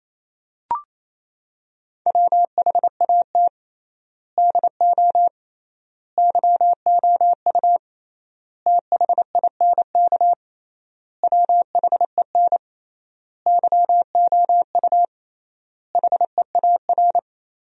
Morse code, heard as a keyed tone, about 28 wpm, 700 Hz. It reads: WHAT DO YOU THINK WHEN YOU HEAR
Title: 3x Word-Spacing - Sentences from Top 300 Words - 28wpm